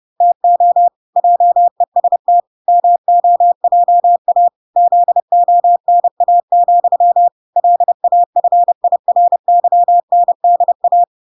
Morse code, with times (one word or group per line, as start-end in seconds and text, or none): 0.2-0.9 TO
1.2-2.4 JEST
2.7-4.5 MOJA
4.8-7.3 ZONA,
7.6-11.0 LAFIRYNDA